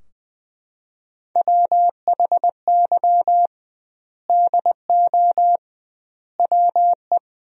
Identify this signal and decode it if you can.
Morse code, heard as a keyed tone, about 20 wpm, 700 Hz. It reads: WHY DO WE
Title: 2x Word-Spacing - Sentences from Top 400 Words - 20wpm